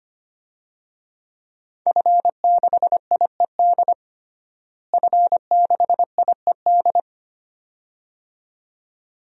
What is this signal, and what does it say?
Morse code, heard as a keyed tone, about 25 wpm, 700 Hz. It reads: F6IED F6IED
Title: DXCC-100 Call Signs - 5 Characters - Repeated 2x - 25wpm